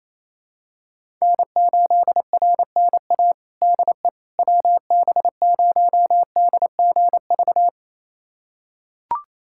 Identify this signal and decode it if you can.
Morse code, heard as a keyed tone, about 28 wpm, 700 Hz. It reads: N8RNA DE WB0DGV
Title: USA Callsign DE Callsign 28wpm